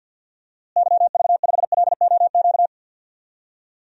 Morse code, heard as a keyed tone, about 53 wpm, 700 Hz. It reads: YV5LOQ
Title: DXCC-100 Call Signs - 6 Characters - Repeated 2x - 50wpm